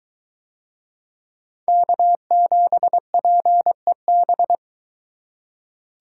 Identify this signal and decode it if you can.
Morse code, heard as a keyed tone, about 21 wpm, 700 Hz. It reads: K7PEB